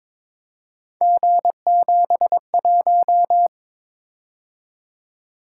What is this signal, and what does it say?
Morse code, heard as a keyed tone, about 22 wpm, 700 Hz. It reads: G71
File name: practice / 22wpm / Three Letter-Number Combinations - 22wpm